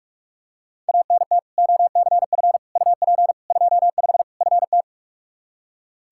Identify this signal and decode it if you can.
Morse code, heard as a keyed tone, about 45 wpm, 700 Hz. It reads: ANT OCF UP 25 FT
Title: Practice QSO - Exchange 2 - Info - 45wpm